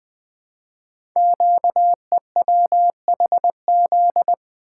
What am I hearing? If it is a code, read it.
Morse code, heard as a keyed tone, about 20 wpm, 700 Hz. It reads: QEWHZ
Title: Code Groups - 5 Letter:Numbers - 20wpm